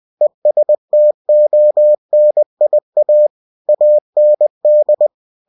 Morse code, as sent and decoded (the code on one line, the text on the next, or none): . ... - --- -. .. .- / .- -. -..
ESTONIA AND